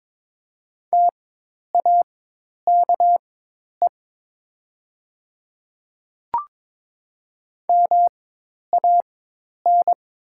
Morse code, heard as a keyed tone, about 22 wpm, spaced as for 10 wpm, 700 Hz. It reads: TAKE MAN
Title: 4x Character Spacing - Top 500 Words - 22wpm